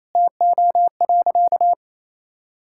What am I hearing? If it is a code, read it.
Morse code, heard as a keyed tone, about 28 wpm, 700 Hz. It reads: TO.
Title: Sentences from Top 1000 Words - Review Part 5 - 28wpm